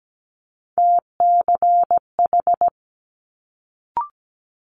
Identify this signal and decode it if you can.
Morse code, heard as a keyed tone, about 17 wpm, 700 Hz. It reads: TCH
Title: Multi-letter Phonemes Quick Fire 17wpm